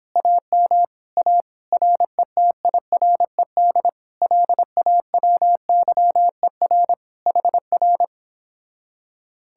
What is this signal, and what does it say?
Morse code, heard as a keyed tone, about 26 wpm, 700 Hz. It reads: AM A RETIRED LAWYER HR
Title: Speed-Racing Common QSO Phrases - 22wpm